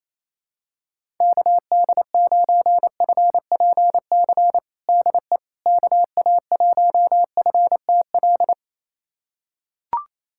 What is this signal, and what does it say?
Morse code, heard as a keyed tone, about 28 wpm, 700 Hz. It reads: KD9FPC DE KA1FTL
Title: USA Callsign DE Callsign - 6 Characters - 28wpm